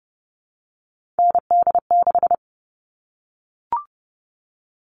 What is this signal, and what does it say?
Morse code, heard as a keyed tone, about 30 wpm, 700 Hz. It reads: ND6
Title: All Cut Numbers - 3-Digit 30wpm